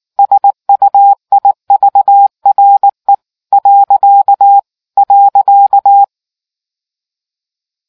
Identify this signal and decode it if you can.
Morse code, heard as a keyed tone, about 18 wpm, 800 Hz. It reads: SUIVRE .